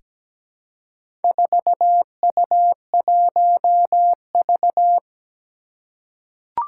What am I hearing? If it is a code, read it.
Morse code, heard as a keyed tone, about 17 wpm, 700 Hz. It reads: 4U1V
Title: Four Challenging Letter-Number Combinations 17wpm